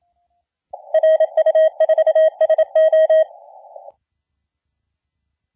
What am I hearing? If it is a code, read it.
Morse code, heard as a keyed tone, about 27 wpm, 630 Hz. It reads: RU4SO
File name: la4c / sac-cw-2017 / CONTEST AUDIO 1100Z